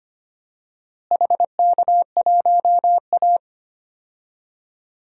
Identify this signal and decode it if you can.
Morse code, heard as a keyed tone, about 25 wpm, 700 Hz. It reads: HK1A
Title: DXCC-100 Call Signs - 4 Characters - Repeated 2x - 25wpm